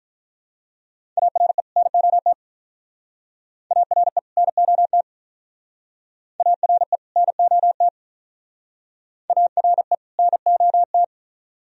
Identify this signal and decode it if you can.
Morse code, heard as a keyed tone, about 44 wpm, 700 Hz. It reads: ARE NOT ARE NOT ARE NOT ARE NOT